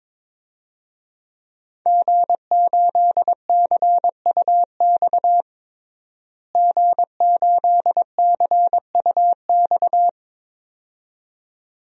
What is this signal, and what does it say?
Morse code, heard as a keyed tone, about 22 wpm, 700 Hz. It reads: G8CUX G8CUX
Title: DXCC-100 Call Signs - 5 Characters - Repeated 2x - 22wpm